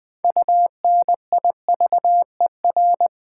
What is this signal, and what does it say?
Morse code, heard as a keyed tone, about 20 wpm, 700 Hz. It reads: UNIVER
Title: Build A Word - 3 Letters At A Time - 9-Letter Words - 20wpm